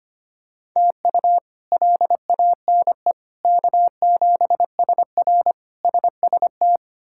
Morse code, heard as a keyed tone, about 25 wpm, 700 Hz. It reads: TU LANE K7SR SST
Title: K1USN SST QSO Practice - 25wpm